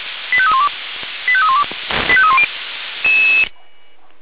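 Five-tone selective call (ZVEI), heard as quick runs of five tones, tones of about 70 ms. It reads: 85312 85312 85310
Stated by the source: ZVEI five-tone selective call received on SDR, from the audio server of fire brigade Feuerwehr Orlamünde.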